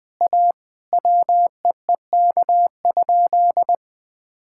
Morse code, heard as a keyed tone, about 21 wpm, 700 Hz. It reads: A WEEK?
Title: Sentences from Top 1000 Words - Review Part 6 - 20wpm